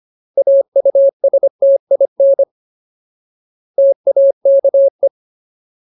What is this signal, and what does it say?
Morse code, heard as a keyed tone, about 25 wpm, 550 Hz. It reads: AUSTIN TAKE
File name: Ultimate Word List / 25WPM Ultimate